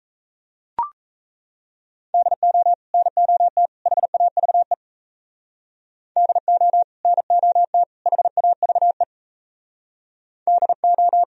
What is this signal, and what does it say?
Morse code, heard as a keyed tone, about 34 wpm, 700 Hz. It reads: DO NOT HAVE DO NOT HAVE DO
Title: Speed-Racing - Sets of 3 Words 28wpm